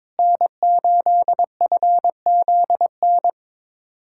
Morse code, heard as a keyed tone, about 22 wpm, 700 Hz. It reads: N8FZN
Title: USA Callsign DE Callsign - 5 Characters - 22wpm